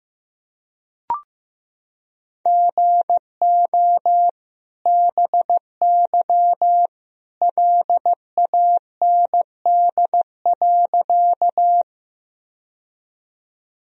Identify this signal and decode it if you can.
Morse code, heard as a keyed tone, about 15 wpm, 700 Hz. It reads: GO BY LAND.
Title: Sentences from Top 1000 Words - Review Part 1 - 15wpm